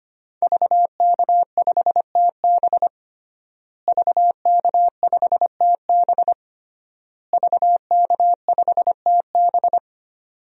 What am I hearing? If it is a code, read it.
Morse code, heard as a keyed tone, about 25 wpm, 700 Hz. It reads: VK5TB VK5TB VK5TB
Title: DXCC-100 Call Signs - 5 Characters - Repeated 3x - 25wpm